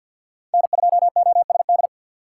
Morse code, heard as a keyed tone, about 51 wpm, 700 Hz. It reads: N2OSD
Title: USA Call Signs - 5 Characters - 6x Character Spacing - 50wpm